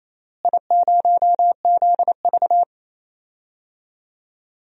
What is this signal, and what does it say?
Morse code, heard as a keyed tone, about 28 wpm, 700 Hz. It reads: I0ZV
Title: DXCC-100 Call Signs - 4 Characters - Repeated 2x - 28wpm